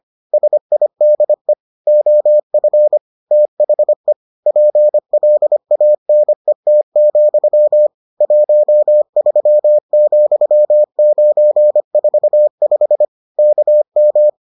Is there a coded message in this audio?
Yes: Morse code from 0.3 to 14.3 s, heard as a keyed tone, about 25 wpm, 600 Hz. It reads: SIDE OF THE PLANET, 13,945 KM